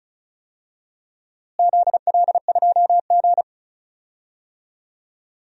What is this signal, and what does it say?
Morse code, heard as a keyed tone, about 34 wpm, 700 Hz. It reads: ZL2G
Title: DXCC-100 Call Signs - 4 Characters - Repeated 3x - 35wpm